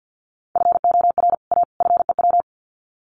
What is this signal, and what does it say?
Morse code, heard as a keyed tone, about 50 wpm, 700 Hz. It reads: FOR A FEW
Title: Sets of 3 Words 50wpm